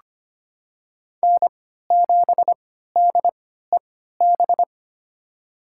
Morse code, heard as a keyed tone, about 25 wpm, spaced as for 14 wpm, 700 Hz. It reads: N7DEB
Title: USA Call Signs - 5 Characters - 3x Character Spacing - 25wpm